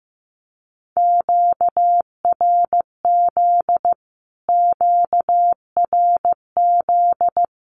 Morse code, heard as a keyed tone, about 15 wpm, 700 Hz. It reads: QRZ QRZ
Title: Q-Codes Repeated 3x 15wpm